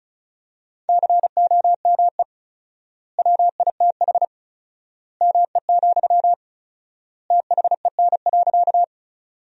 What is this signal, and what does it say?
Morse code, heard as a keyed tone, about 36 wpm, 700 Hz. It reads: COME WITH ME, THEN.
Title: 4x Word-Spacing - Sentences from Top 200 Words - 35wpm